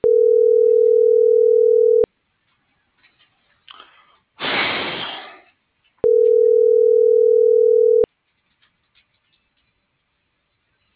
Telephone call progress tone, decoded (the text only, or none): ringback tone